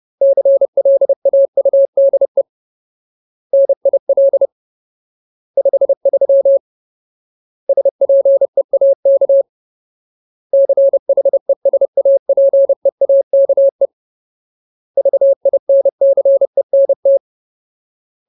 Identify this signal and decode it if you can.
Morse code, heard as a keyed tone, about 30 wpm, 550 Hz. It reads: CLAUDE NIL 53 SPEAK CHESAPEAKE VINCENT
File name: Ultimate Word List / 30WPM Ultimate